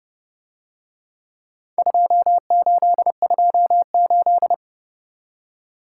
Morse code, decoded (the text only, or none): 2828